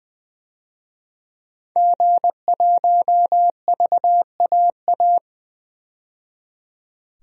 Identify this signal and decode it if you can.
Morse code, heard as a keyed tone, about 20 wpm, 700 Hz. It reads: G1VAA
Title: DXCC-100 Call Signs - 5 Characters - 20wpm